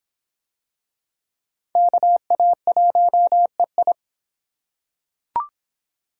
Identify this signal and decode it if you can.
Morse code, heard as a keyed tone, about 25 wpm, 700 Hz. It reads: KA1EI